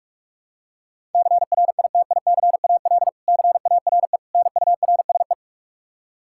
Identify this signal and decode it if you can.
Morse code, heard as a keyed tone, about 41 wpm, 700 Hz. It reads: CRITICAL CARE NURSE
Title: Practice QSO - Exchange 3 - More Info - 45wpm